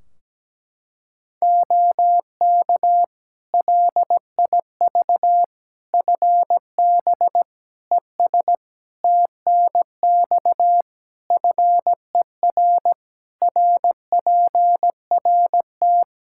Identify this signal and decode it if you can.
Morse code, heard as a keyed tone, about 17 wpm, 700 Hz. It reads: OK LIV FB ES TNX FER RPRT
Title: Practice QSO - Exchange 2 - Info - 17wpm